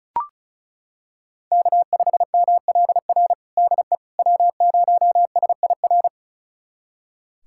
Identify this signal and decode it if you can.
Morse code, heard as a keyed tone, about 36 wpm, 700 Hz. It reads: K5MLR DE W0SIR
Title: USA Callsign DE Callsign - 5 Characters - 35wpm